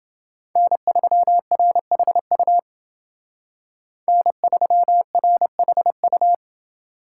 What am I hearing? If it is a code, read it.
Morse code, heard as a keyed tone, about 29 wpm, 700 Hz. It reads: N3RHU N3RHU